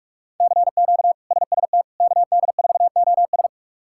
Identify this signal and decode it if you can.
Morse code, heard as a keyed tone, about 44 wpm, 700 Hz. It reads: CQ SST KD4OS
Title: K1USN SST QSO Practice - 45wpm